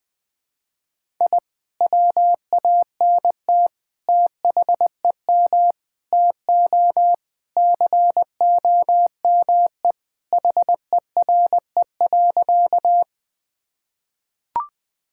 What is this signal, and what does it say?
Morse code, heard as a keyed tone, about 20 wpm, 700 Hz. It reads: I WANT THEM TO COME HERE.
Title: Sentences with 2-Word Phrases - Spoken Beginning - 20wpm